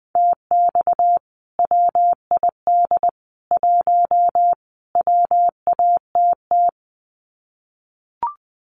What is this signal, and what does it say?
Morse code, heard as a keyed tone, about 19 wpm, 700 Hz. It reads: TX WID 1 WATT
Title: Common QSO Phrases 20wpm